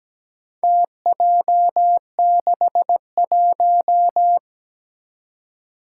Morse code, heard as a keyed tone, about 17 wpm, 700 Hz. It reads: TJ61
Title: Four Challenging Letter-Number Combinations 17wpm